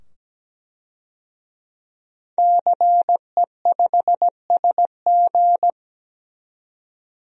Morse code, heard as a keyed tone, about 17 wpm, 700 Hz. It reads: CE5SG
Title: DXCC-100 Call Signs - 5 Characters - 17wpm